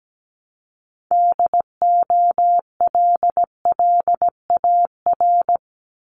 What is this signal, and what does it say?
Morse code, heard as a keyed tone, about 17 wpm, 700 Hz. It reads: DOLLAR
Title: Multi-letter Phonemes Repeated 3x 17wpm